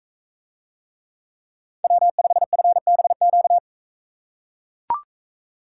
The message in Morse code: W5FBQ

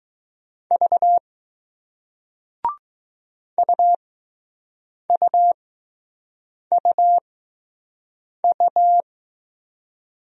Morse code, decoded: V UUUU

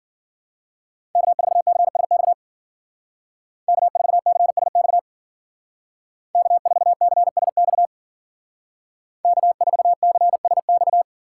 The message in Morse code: K4CSX K4CSX K4CSX K4CSX

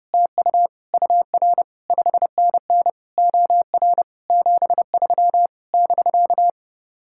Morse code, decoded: TU UR 5NN OR 73 <BK>

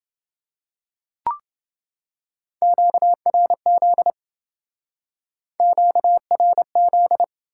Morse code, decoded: QRZ QRZ

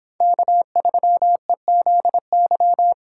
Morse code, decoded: K3EZY

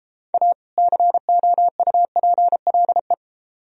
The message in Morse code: A COUPLE

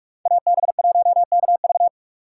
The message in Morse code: AB1KV

